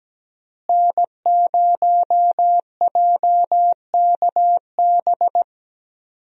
Morse code, heard as a keyed tone, about 17 wpm, 700 Hz. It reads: N0JKB
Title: USA Call Signs - 5 Characters - Repeated 2x - 17wpm